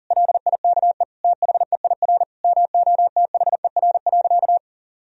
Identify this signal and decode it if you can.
Morse code, heard as a keyed tone, about 40 wpm, 700 Hz. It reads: LIKE THEIR MOTHER.
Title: Sentences from Top 1000 Words - Review Part 3 - 40wpm